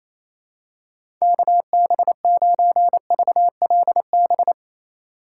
Morse code, heard as a keyed tone, about 28 wpm, 700 Hz. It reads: KB9VLB
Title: USA Call Signs - 6x Character Spacing - 28wpm